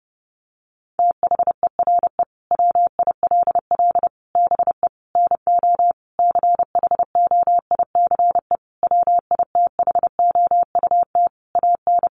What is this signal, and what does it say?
Morse code, heard as a keyed tone, about 30 wpm, 700 Hz. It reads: THERE WILL BE NO CHOICE WITHOUT AN
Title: Sentences from Top 1500 Words 30wpm